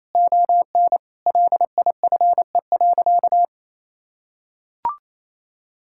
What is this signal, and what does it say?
Morse code, heard as a keyed tone, about 28 wpm, 700 Hz. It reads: ON LIFE.
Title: Sentences from Top 1600 Words 28wpm